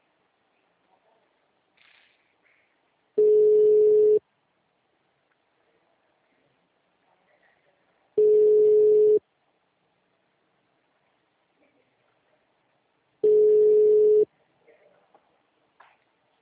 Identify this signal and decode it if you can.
Ringback tone (Europe, 425 Hz).
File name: out- -2010- - - .13024